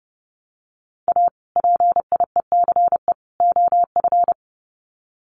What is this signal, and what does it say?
Morse code, heard as a keyed tone, about 30 wpm, 700 Hz. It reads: A PIECE OF
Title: Sets of 3 Words 30wpm